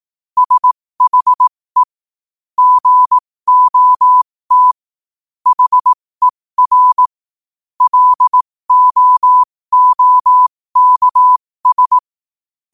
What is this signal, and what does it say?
Morse code, heard as a keyed tone, about 18 wpm, spaced as for 15 wpm, 1000 Hz. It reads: SHE GOT HER LOOKS
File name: qotdfiles / 15wpm